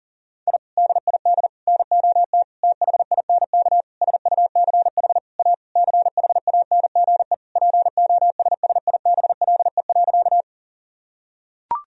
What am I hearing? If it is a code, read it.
Morse code, heard as a keyed tone, about 40 wpm, 700 Hz. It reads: I DID NOT THINK SUCH A CHANGE POSSIBLE.
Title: Sentences from Top 1000 Words - Review Part 1 - 40wpm